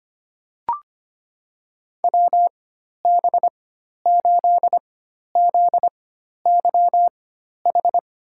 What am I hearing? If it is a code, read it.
Morse code, heard as a keyed tone, about 25 wpm, spaced as for 12 wpm, 700 Hz. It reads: WB8ZYH
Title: USA Call Signs - 6 Characters - 4x Character Spacing - 25wpm